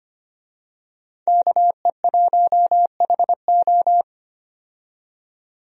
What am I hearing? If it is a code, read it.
Morse code, heard as a keyed tone, about 25 wpm, 690 Hz. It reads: KE1HO
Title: USA Call Signs - 5 Characters - 25wpm